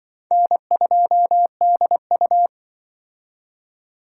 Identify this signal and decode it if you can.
Morse code, heard as a keyed tone, about 24 wpm, 700 Hz. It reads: N2DU
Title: Speed Racing - USA 1x2 Call Signs 20wpm